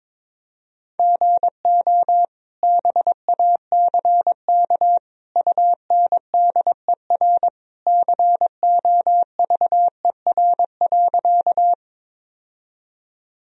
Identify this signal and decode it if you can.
Morse code, heard as a keyed tone, about 22 wpm, 700 Hz. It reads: GO BACK UNDER COVER.
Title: Sentences from Top 1000 Words - Review Part 1 - 22wpm